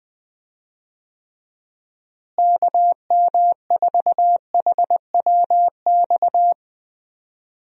Morse code, heard as a keyed tone, about 21 wpm, 700 Hz. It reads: KM4HWX
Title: USA Call Signs - 6 Characters - Repeated 2x - 20wpm